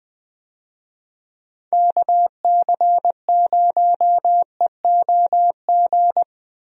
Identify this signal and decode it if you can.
Morse code, heard as a keyed tone, about 20 wpm, 700 Hz. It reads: KC0EOG
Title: USA Call Signs - 6 Characters - Repeated 3x - 20wpm